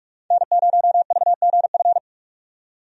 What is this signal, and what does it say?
Morse code, heard as a keyed tone, about 46 wpm, 700 Hz. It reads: N0VGF
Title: USA Call Signs - 5 Characters - 45wpm